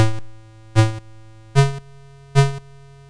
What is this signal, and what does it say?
There is no code.